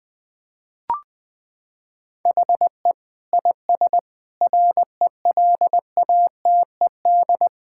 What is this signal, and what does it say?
Morse code, heard as a keyed tone, about 21 wpm, 700 Hz. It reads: HE IS RELATED
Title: Sentences from Top 1000 Words - Review Part 5 - 20wpm